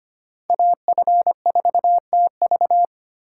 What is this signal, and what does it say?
Morse code, heard as a keyed tone, about 29 wpm, 700 Hz. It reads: AF4TV
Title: Speed Racing - USA 2x2 Call Signs 25wpm